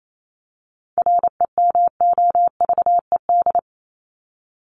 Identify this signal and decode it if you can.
Morse code, heard as a keyed tone, about 28 wpm, 700 Hz. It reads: REMOVED
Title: ED Suffix - Top 100 Words with Base Words 28wpm